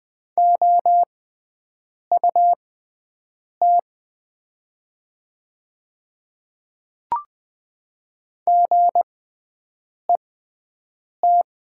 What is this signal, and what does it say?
Morse code, heard as a keyed tone, about 20 wpm, spaced as for 7 wpm, 700 Hz. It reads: OUT GET
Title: 6x Character Spacing - Top 500 Words - 20wpm